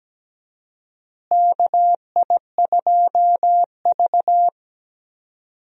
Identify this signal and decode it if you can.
Morse code, heard as a keyed tone, about 17 wpm, 700 Hz. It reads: KI2V
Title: USA Call Signs - 4 Characters - Repeated 3x - 17wpm